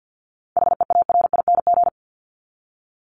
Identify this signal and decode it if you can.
Morse code, heard as a keyed tone, about 48 wpm, 700 Hz. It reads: HEARING
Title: Top 1751-2000 Words 50wpm